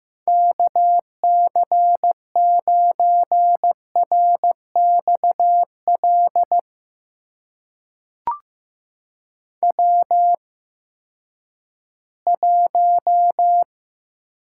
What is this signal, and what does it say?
Morse code, heard as a keyed tone, about 15 wpm, spaced as for 10 wpm, 700 Hz. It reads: KC9RXL W1
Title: USA Call Signs - 8x Character Spacing - 15wpm